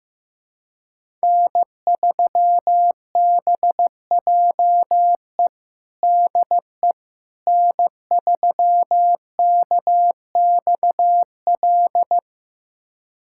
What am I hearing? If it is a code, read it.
Morse code, heard as a keyed tone, about 15 wpm, 700 Hz. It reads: N3BJE DE N3KXL
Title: USA Callsign DE Callsign - 5 Characters - 15wpm